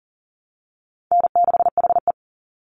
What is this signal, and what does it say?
Morse code, heard as a keyed tone, about 40 wpm, 700 Hz. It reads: N6HE
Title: Easy Call Signs Repeated 2x - 40wpm